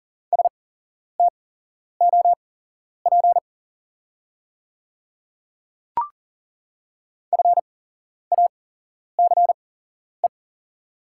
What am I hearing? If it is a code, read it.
Morse code, heard as a keyed tone, about 40 wpm, spaced as for 11 wpm, 690 Hz. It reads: STOP FACE
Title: 8x Character Spacing - Top 500 Words - 40wpm